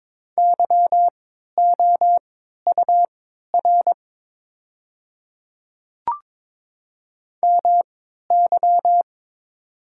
Morse code, heard as a keyed tone, about 22 wpm, spaced as for 12 wpm, 700 Hz. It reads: YOUR MY
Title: 3x Character Spacing - Top 100 Words - 22wpm